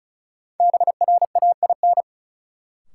Morse code, heard as a keyed tone, about 36 wpm, 700 Hz. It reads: BRAIN
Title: Sentences from Top 1100 Words - New Word Drill Alt - 35wpm